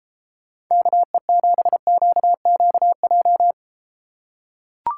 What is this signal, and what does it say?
Morse code, heard as a keyed tone, about 32 wpm, 700 Hz. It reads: KE7QQJ